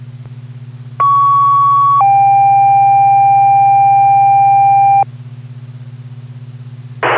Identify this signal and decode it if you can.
Two-tone page, heard as a one-second tone, then a three-second tone, about 1120 then 790 Hz. It is heard from 1.0 to 5.0 s.